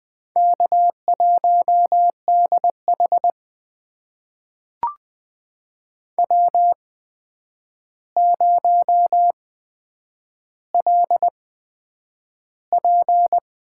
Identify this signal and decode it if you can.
Morse code, heard as a keyed tone, about 20 wpm, spaced as for 9 wpm, 700 Hz. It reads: K1DH W0LP
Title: USA Call Signs - 4 Characters - 8x Character Spacing - 20wpm